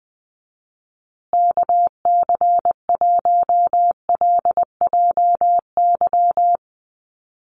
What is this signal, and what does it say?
Morse code, heard as a keyed tone, about 20 wpm, 700 Hz. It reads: KC1LJY
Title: USA Call Signs Repeated 3x 20wpm